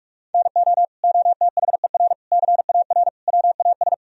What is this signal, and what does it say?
Morse code, heard as a keyed tone, about 46 wpm, 700 Hz. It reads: NO OTHER CAR WAS
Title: Sentences from Top 1000 Words - Review Part 4 - 45wpm